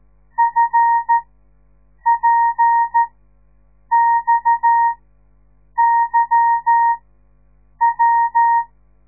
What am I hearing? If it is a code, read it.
Morse code, heard as a keyed tone, about 13 wpm, spaced as for 7 wpm, 930 Hz. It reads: FPXYW
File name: radiomorningstar / morse / les 9